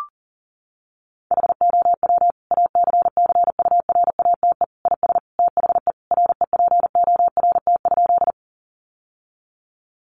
Morse code, heard as a keyed tone, about 40 wpm, 700 Hz. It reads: HOW ACCURATE IS THE REPORT?
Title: Sentences from Top 1200 Words 40wpm